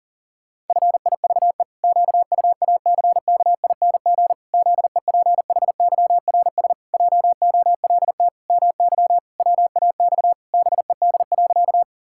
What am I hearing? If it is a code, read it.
Morse code, heard as a keyed tone, about 40 wpm, 700 Hz. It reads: FIVE QUACKING ZEPHYRS JOLT MY WAX BED.
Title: Pangram Challenge v2 - 40wpm